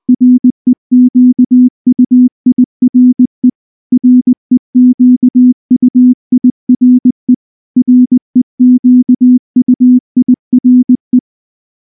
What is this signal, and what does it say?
Morse code, heard as a keyed tone, about 19 wpm, 250 Hz. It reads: REQUIRE REQUIRE REQUIRE
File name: morse / morse-training-20WPM-3